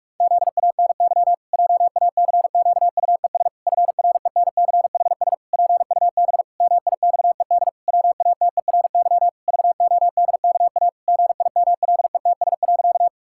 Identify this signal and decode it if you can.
Morse code, heard as a keyed tone, about 46 wpm, 700 Hz. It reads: ZANY JACQUES FRENCHS PUB MIXED WATERY VODKA GIMLETS.
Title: Pangram Challenge v2 - 45wpm